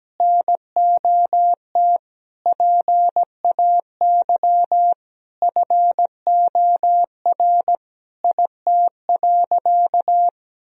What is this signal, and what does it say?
Morse code, heard as a keyed tone, about 17 wpm, 700 Hz. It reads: NOT PAY FOR IT.
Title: Sentences from Top 1000 Words - Review Part 3 - 17wpm